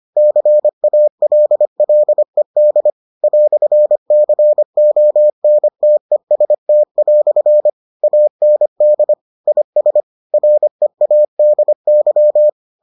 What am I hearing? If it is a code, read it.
Morse code, heard as a keyed tone, about 25 wpm, 600 Hz. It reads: CALLED "CONTEST" AND IS READY